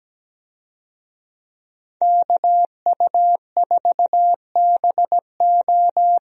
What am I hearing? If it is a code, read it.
Morse code, heard as a keyed tone, about 17 wpm, 700 Hz. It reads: KU4BO
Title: USA Call Signs - 5 Characters - 17wpm